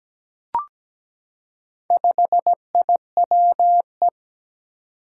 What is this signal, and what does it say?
Morse code, heard as a keyed tone, about 17 wpm, 700 Hz. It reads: 5IWE
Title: Four Challenging Letter-Number Combinations 17wpm